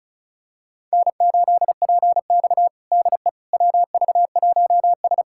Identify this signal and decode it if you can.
Morse code, heard as a keyed tone, about 36 wpm, 700 Hz. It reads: N8PX DE WV1S
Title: USA Callsign DE Callsign - 4 Characters - 35wpm